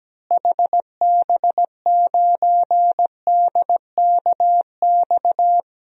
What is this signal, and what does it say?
Morse code, heard as a keyed tone, about 17 wpm, 700 Hz. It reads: HB9DKX
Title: DXCC-100 Call Signs - 6 Characters - Repeated 3x - 17wpm